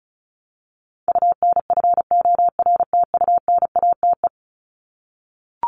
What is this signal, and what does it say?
Morse code, heard as a keyed tone, about 34 wpm, 700 Hz. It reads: UNFORTUNATE
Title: LY Suffix - Top 100 Words with Base Words 35wpm